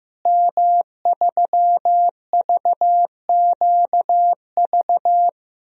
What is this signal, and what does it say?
Morse code, heard as a keyed tone, about 15 wpm, 700 Hz. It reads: M3VQV
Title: DXCC-100 Call Signs - 5 Characters - Repeated 3x - 15wpm